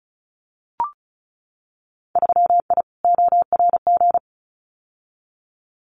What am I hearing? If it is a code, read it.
Morse code, heard as a keyed tone, about 34 wpm, 700 Hz. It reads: 3I ORG
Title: Winter Field Day Exchanges Encore 35wpm